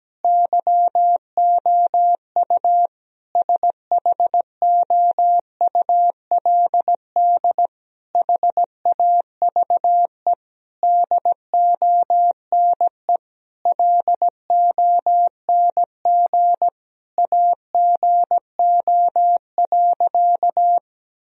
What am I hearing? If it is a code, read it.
Morse code, heard as a keyed tone, about 17 wpm, 700 Hz. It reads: YOU SHOULD HAVE DONE LONG AGO.
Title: Sentences from Top 1000 Words - Review Part 3 - 17wpm